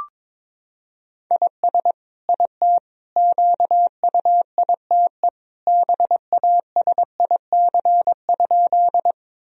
Morse code, heard as a keyed tone, about 22 wpm, 700 Hz. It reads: IS IT QUITE BASIC?
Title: Sentences from Top 1000 Words - Review Part 3 - 22wpm